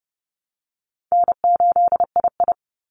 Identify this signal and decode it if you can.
Morse code, heard as a keyed tone, about 30 wpm, 700 Hz. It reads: N8II
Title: Easy Call Signs Repeated 2x - 30wpm